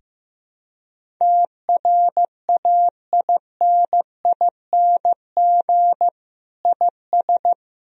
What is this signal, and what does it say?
Morse code, heard as a keyed tone, about 15 wpm, 700 Hz. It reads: TRAINING IS
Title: Sentences from Top 1000 Words - Review Part 4 - 15wpm